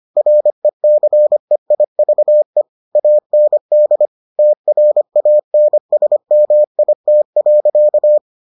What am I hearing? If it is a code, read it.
Morse code, heard as a keyed tone, about 25 wpm, 600 Hz. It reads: RECEIVE AND TRANSMIT.